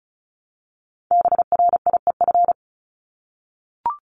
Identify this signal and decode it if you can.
Morse code, heard as a keyed tone, about 34 wpm, 700 Hz. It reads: BRIEF
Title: LY Suffix - Top 100 Words with Base Words 35wpm